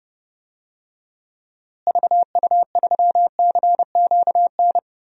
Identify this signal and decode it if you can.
Morse code, heard as a keyed tone, about 31 wpm, 700 Hz. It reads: VU3CQN
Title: DXCC-100 Call Signs - 6 Characters - Repeated 3x - 30wpm